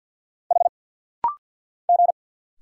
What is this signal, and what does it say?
Morse code, heard as a keyed tone, about 48 wpm, 690 Hz. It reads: H G
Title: Single Letter-Number - Rapid-Fire 50wpm